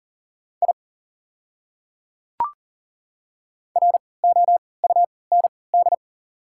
Morse code, heard as a keyed tone, about 38 wpm, spaced as for 22 wpm, 700 Hz. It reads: I ROUND
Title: 3x Character Spacing - Top 200 Words - 40wpm